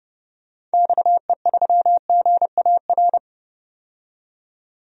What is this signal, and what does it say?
Morse code, heard as a keyed tone, about 31 wpm, 700 Hz. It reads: XE3GAR
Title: DXCC-100 Call Signs - 6 Characters - 30wpm